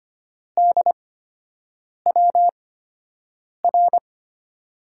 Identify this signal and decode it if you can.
Morse code, heard as a keyed tone, about 25 wpm, spaced as for 7 wpm, 700 Hz. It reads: DWR
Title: USA Call Signs - 5 Characters - 8x Character Spacing - 25wpm